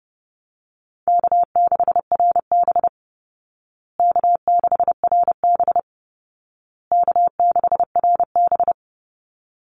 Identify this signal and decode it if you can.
Morse code, heard as a keyed tone, about 30 wpm, 700 Hz. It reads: K6RB K6RB K6RB
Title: Call Signs Repeated 2x - 30wpm